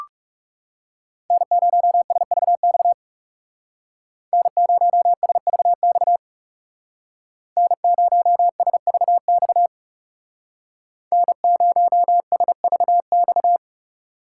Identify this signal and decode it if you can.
Morse code, heard as a keyed tone, about 39 wpm, 700 Hz. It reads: N0SVX N0SVX N0SVX N0SVX